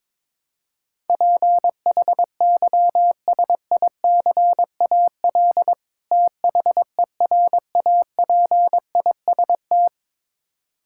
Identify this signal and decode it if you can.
Morse code, heard as a keyed tone, about 22 wpm, 700 Hz. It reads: PHYSICAL THERAPIST
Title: Practice QSO - Exchange 3 - More Info - 22wpm